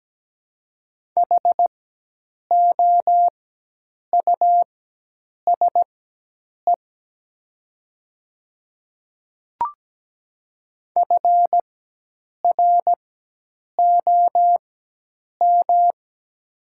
Morse code, heard as a keyed tone, about 17 wpm, spaced as for 8 wpm, 700 Hz. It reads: HOUSE FROM
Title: 4x Character Spacing - Top 200 Words - 17wpm